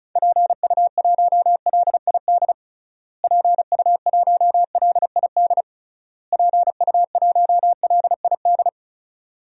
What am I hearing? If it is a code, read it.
Morse code, heard as a keyed tone, about 36 wpm, 700 Hz. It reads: PU1LID PU1LID PU1LID
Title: DXCC-100 Call Signs - 6 Characters - Repeated 3x - 35wpm